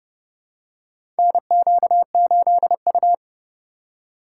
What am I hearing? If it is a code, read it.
Morse code, heard as a keyed tone, about 30 wpm, 700 Hz. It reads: NQ8U
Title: USA Call Signs - 4 Characters - 3x Character Spacing - 30wpm